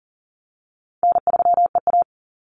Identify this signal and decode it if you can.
Morse code, heard as a keyed tone, about 40 wpm, 700 Hz. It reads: N3EA
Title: Easy Call Signs - 40wpm